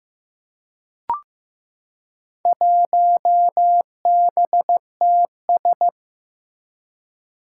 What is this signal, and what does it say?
Morse code, heard as a keyed tone, about 15 wpm, 700 Hz. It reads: 1BTS
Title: Four Challenging Letter-Number Combinations 15wpm